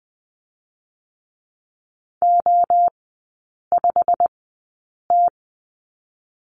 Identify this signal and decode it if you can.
Morse code, heard as a keyed tone, about 20 wpm, 700 Hz. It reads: O 5 T
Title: 178 Lesson 29e - Slash - Three Characters 20wpm